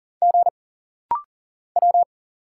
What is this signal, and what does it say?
Morse code, heard as a keyed tone, about 40 wpm, 700 Hz. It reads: G W
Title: Single Letter-Number - Rapid-Fire 40wpm